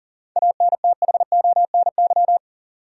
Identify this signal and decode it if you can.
Morse code, heard as a keyed tone, about 40 wpm, 700 Hz. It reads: ANTHONY
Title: USA Common Names - 40wpm